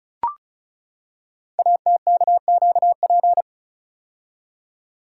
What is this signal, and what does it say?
Morse code, heard as a keyed tone, about 36 wpm, 700 Hz. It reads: ATKQP
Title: Code Groups - 5 Letters - 35wpm